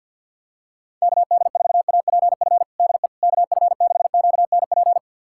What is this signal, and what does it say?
Morse code, heard as a keyed tone, about 53 wpm, 700 Hz. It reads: KD4APF DE KF6QNP
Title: USA Callsign DE Callsign - 6 Characters - 50wpm